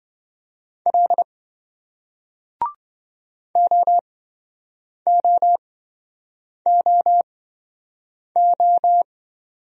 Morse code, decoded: L OOOO

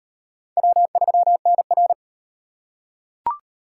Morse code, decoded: W3NR

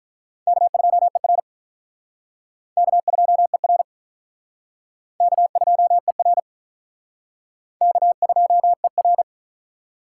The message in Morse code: K2ER K2ER K2ER K2ER

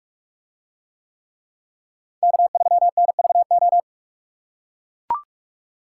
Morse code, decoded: K3NVO

